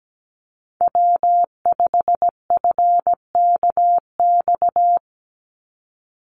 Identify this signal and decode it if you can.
Morse code, heard as a keyed tone, about 17 wpm, 700 Hz. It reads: W5FKX
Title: Call Signs - 17wpm